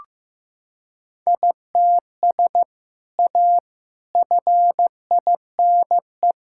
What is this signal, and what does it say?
Morse code, heard as a keyed tone, about 15 wpm, 700 Hz. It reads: ITS A FINE
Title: Sentences from Top 1000 Words - Review Part 6 - 15wpm